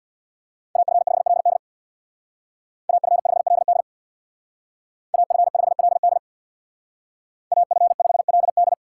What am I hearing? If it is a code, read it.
Morse code, heard as a keyed tone, about 60 wpm, 690 Hz. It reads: AF5LD AF5LD AF5LD AF5LD
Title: Speed Racing - USA 2x2 Call Signs 50wpm